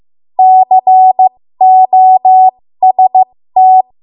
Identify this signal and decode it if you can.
Morse code, heard as a keyed tone, about 15 wpm, spaced as for 13 wpm, 750 Hz. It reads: COST
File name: Morse / 13WPM